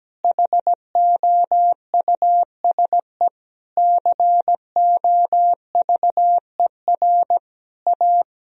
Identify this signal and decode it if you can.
Morse code, heard as a keyed tone, about 17 wpm, 700 Hz. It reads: HOUSE COVER A
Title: Sentences from Top 1000 Words - Review Part 2 - 17wpm